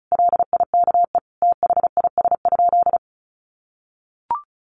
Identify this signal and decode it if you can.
Morse code, heard as a keyed tone, about 34 wpm, 700 Hz. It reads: LIKE THIS?
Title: Sentences from Top 1400 Words 35wpm